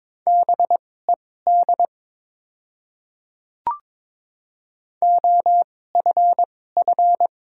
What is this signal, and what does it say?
Morse code, heard as a keyed tone, about 22 wpm, spaced as for 16 wpm, 700 Hz. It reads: BED OFF
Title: 2x Character Spacing - Top 500 Words - 22wpm